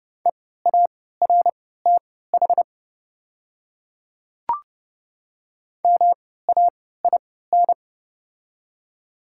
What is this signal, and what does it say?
Morse code, heard as a keyed tone, about 30 wpm, spaced as for 17 wpm, 700 Hz. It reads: EARTH MAIN
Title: 3x Character Spacing - Top 300 Words - 30wpm